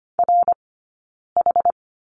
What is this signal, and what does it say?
Morse code, heard as a keyed tone, about 25 wpm, 700 Hz. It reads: R H
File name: Lesson 15d - Number 5 - Two Characters 25wpm